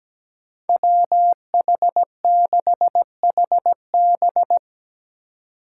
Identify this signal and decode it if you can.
Morse code, heard as a keyed tone, about 17 wpm, 700 Hz. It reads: WH6HB
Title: USA Call Signs - 5 Characters - 8x Character Spacing - 17wpm